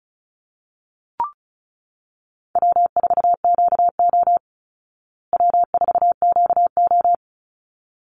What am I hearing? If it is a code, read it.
Morse code, heard as a keyed tone, about 34 wpm, 700 Hz. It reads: W4QO W4QO
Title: Easy Call Signs Repeated 2x - 35wpm